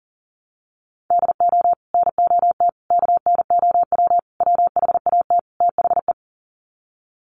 Morse code, heard as a keyed tone, about 40 wpm, 700 Hz. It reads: DO NOT KNOW WHAT THE
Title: Sets of 5 Words 40wpm